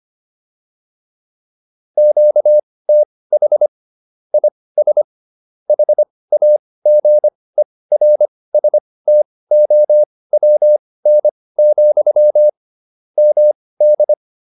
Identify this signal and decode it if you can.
Morse code, heard as a keyed tone, about 25 wpm, spaced as for 18 wpm, 600 Hz. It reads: QTH IS HAGERSTOWN, MD
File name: QSO307 18